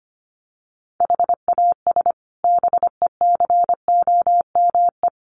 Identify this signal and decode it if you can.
Morse code, heard as a keyed tone, about 25 wpm, 700 Hz. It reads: HAS BECOME
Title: Sets of 5 Words Encore 25wpm